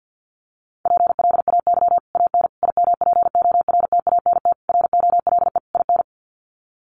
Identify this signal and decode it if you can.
Morse code, heard as a keyed tone, about 49 wpm, 700 Hz. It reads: PLAY AN IMPORTANT ROLE IN
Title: Sets of 5 Words Encore 50wpm